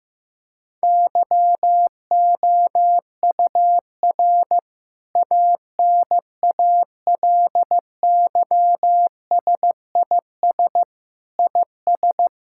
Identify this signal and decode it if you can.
Morse code, heard as a keyed tone, about 15 wpm, 700 Hz. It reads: YOUR ANALYSIS IS
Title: Sentences from Top 1000 Words - Review Part 5 - 15wpm